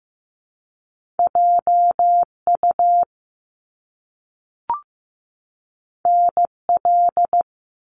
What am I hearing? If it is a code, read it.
Morse code, heard as a keyed tone, about 15 wpm, 700 Hz. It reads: JU NL
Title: Top Two Letter Combinations in Words 15wpm